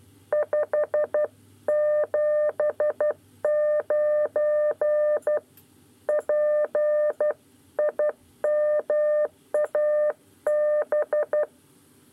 Morse code, heard as a keyed tone, about 12 wpm, 600 Hz. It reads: 579 P IMAB
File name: 9 lezione cw 2